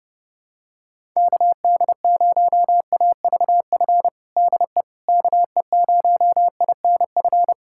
Morse code, heard as a keyed tone, about 32 wpm, 700 Hz. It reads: KD0AVF DE KE0INF
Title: USA Callsign DE Callsign 30wpm